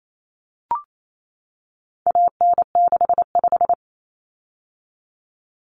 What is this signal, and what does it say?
Morse code, heard as a keyed tone, about 28 wpm, 700 Hz. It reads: AN65
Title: Common Cut Numbers - 4-Digit 28wpm